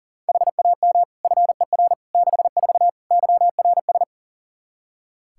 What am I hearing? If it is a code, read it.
Morse code, heard as a keyed tone, about 40 wpm, 690 Hz. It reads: HAM FER 64 YRS
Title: Practice QSO - Exchange 3 - More Info - 40wpm